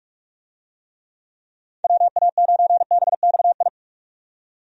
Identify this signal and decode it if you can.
Morse code, heard as a keyed tone, about 46 wpm, 700 Hz. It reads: WA9BXI
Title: USA Call Signs - 6 Characters - 45wpm